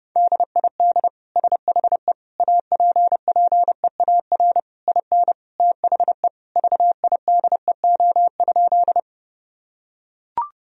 Morse code, heard as a keyed tone, about 30 wpm, 700 Hz. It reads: DID SHE APPEAR IN THE VIDEO?
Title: Sentences from Top 1000 Words - Review Part 4 - 30wpm